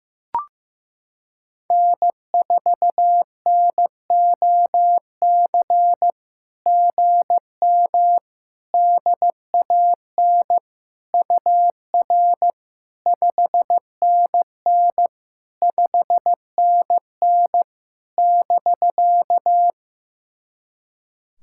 Morse code, heard as a keyed tone, about 15 wpm, 700 Hz. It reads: N4NOC GM DAN UR 5NN 5NN <BK>
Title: SOTA QSO Practice 15wpm